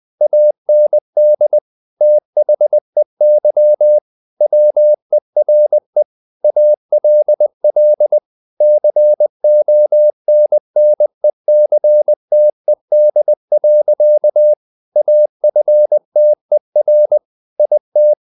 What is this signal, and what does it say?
Morse code, heard as a keyed tone, about 20 wpm, 600 Hz. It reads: AND THEY WERE ALL CONNECTED. AFTER IT
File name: the old man and the sea / the old man and the sea part 1 20wpm